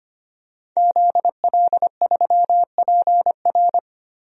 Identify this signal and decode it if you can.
Morse code, heard as a keyed tone, about 25 wpm, 700 Hz. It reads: ZL3PR
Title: DXCC-100 Call Signs - 5 Characters - 25wpm